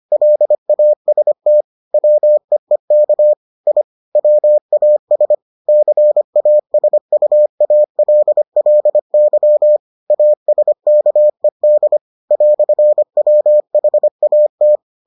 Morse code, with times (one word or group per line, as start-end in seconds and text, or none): 0.1-1.6 LAST
1.9-3.3 WEEK
3.7-3.8 I
4.2-5.3 WAS
5.7-9.8 CASUALLY
10.1-12.0 ASKED
12.3-14.8 "WHAT